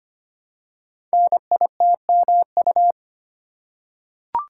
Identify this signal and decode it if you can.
Morse code, heard as a keyed tone, about 25 wpm, 700 Hz. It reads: NITMU
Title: Code Groups - 5 Letter:Numbers - 25wpm